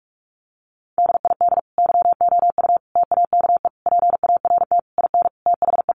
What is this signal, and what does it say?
Morse code, heard as a keyed tone, about 44 wpm, 700 Hz. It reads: DID YOU TAKE PART IN THE
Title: Sentences from Top 1200 Words 45wpm